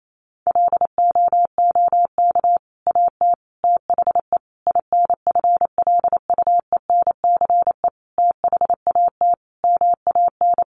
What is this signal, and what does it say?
Morse code, heard as a keyed tone, about 28 wpm, 700 Hz. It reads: LOOK AT THE INFLUENCE THAT MAN
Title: Sentences from Top 1100 Words 28wpm